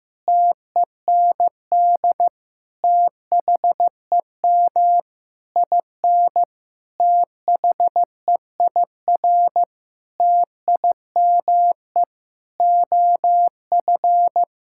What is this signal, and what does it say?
Morse code, heard as a keyed tone, about 15 wpm, 700 Hz. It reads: TEND THEM IN THEIR TIME OF
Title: Sentences from Top 1000 Words - Review Part 5 - 15wpm